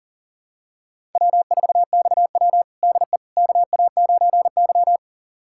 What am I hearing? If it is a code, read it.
Morse code, heard as a keyed tone, about 40 wpm, 700 Hz. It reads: W4XW DE KA9Y
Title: USA Callsign DE Callsign - 4 Characters - 40wpm